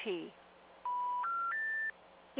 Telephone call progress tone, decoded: special information tone